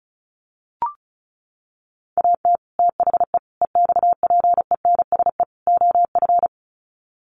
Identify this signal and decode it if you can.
Morse code, heard as a keyed tone, about 34 wpm, 700 Hz. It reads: AT THE EXPENSE OF
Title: Sets of 4 Words 35wpm